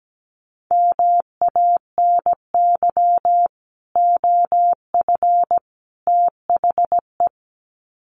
Morse code, heard as a keyed tone, about 17 wpm, 700 Hz. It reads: MANY OF THE
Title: Sets of 3 Words 17wpm